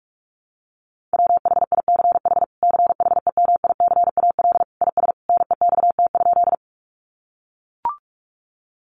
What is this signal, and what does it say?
Morse code, heard as a keyed tone, about 44 wpm, 700 Hz. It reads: WHICH CHEMICAL IS NEXT?
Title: Sentences from Top 1200 Words 45wpm